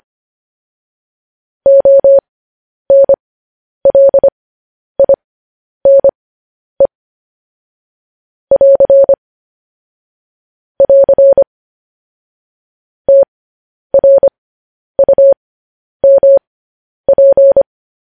Morse code, heard as a keyed tone, about 25 wpm, spaced as for 10 wpm, 550 Hz. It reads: ONLINE <AR> <AR> TRUMP
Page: news-headlines-in-morse-code-at-10-wpm